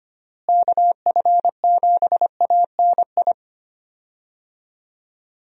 Morse code, heard as a keyed tone, about 25 wpm, 700 Hz. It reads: KF7ANI
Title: USA Call Signs - 6 Characters - Repeated 2x - 25wpm